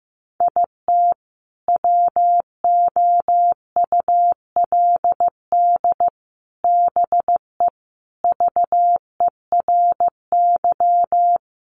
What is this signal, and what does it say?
Morse code, heard as a keyed tone, about 14 wpm, 700 Hz. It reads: IT WOULD BE VERY
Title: Sentences from Top 200 Words 15wpm